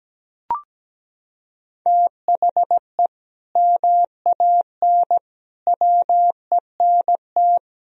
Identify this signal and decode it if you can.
Morse code, heard as a keyed tone, about 17 wpm, 700 Hz. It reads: THE MAN WENT
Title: Sentences from Top 1000 Words - Review Part 1 - 17wpm